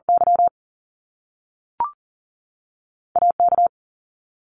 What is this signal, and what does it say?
Morse code, heard as a keyed tone, about 40 wpm, 700 Hz. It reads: Y AK
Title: Top Two Letter Combinations in Words 40wpm